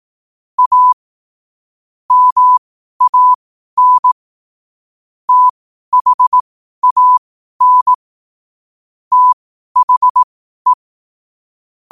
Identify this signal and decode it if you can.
Morse code, heard as a keyed tone, about 18 wpm, spaced as for 12 wpm, 1000 Hz. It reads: A MAN THAN THE
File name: qotdfiles / 12wpm